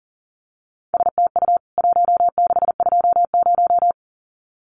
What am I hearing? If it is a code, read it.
Morse code, heard as a keyed tone, about 40 wpm, 700 Hz. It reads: STU 1620
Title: CWT Exchange 40wpm